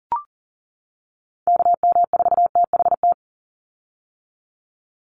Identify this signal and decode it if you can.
Morse code, heard as a keyed tone, about 40 wpm, 700 Hz. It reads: KM4THT
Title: USA Call Signs 40wpm